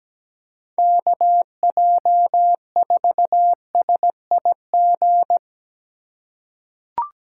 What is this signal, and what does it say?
Morse code, heard as a keyed tone, about 17 wpm, 700 Hz. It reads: KJ4SIG
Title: USA Call Signs - 6 Characters - 8x Character Spacing - 17wpm